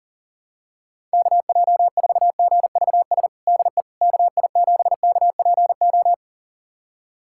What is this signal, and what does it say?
Morse code, heard as a keyed tone, about 40 wpm, 700 Hz. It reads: KJ4GVS DE KI7KPO
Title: USA Callsign DE Callsign - 6 Characters - 40wpm